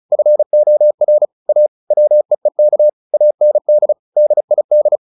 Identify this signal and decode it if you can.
Morse code, heard as a keyed tone, about 34 wpm, 600 Hz. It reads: FOR A WEEK AND DID
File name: the old man and the sea / the old man and the sea 1 35wpm